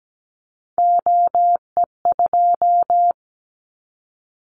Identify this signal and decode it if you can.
Morse code, heard as a keyed tone, about 17 wpm, 700 Hz. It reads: OE2
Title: Callsign Prefixes for Top 25 DXCC - Repeated 3x - 17wpm